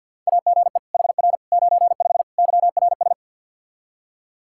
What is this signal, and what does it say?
Morse code, heard as a keyed tone, about 47 wpm, 700 Hz. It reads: AGE HR 95 YRS
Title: Practice QSO - Exchange 3 - More Info - 50wpm